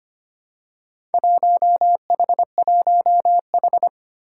Morse code, heard as a keyed tone, about 25 wpm, 700 Hz. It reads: 1H1H
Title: Four Challenging Letter-Number Combinations 25wpm